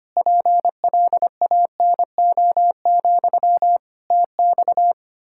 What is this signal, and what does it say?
Morse code, heard as a keyed tone, about 25 wpm, 700 Hz. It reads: PLANO, TX
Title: K1USN SST QSO Practice 25wpm